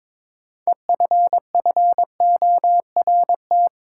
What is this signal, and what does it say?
Morse code, heard as a keyed tone, about 21 wpm, 700 Hz. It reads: EFFORT
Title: Sentences from Top 1200 Words - New Word Drill - 22wpm